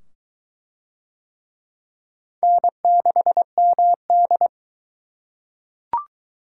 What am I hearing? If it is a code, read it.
Morse code, heard as a keyed tone, about 21 wpm, 700 Hz. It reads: N6MD